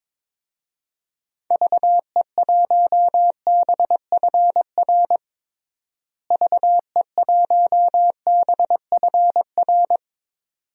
Morse code, heard as a keyed tone, about 22 wpm, 700 Hz. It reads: VE1BFR VE1BFR
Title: DXCC-100 Call Signs - 6 Characters - Repeated 2x - 22wpm